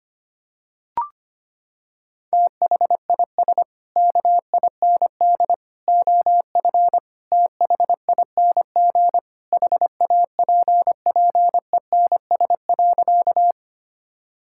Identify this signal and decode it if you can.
Morse code, heard as a keyed tone, about 25 wpm, 700 Hz. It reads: THIS KIND OF THING HAPPENS.
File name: practice / 25wpm / Sentences with 3-Word Phrases - 25wpm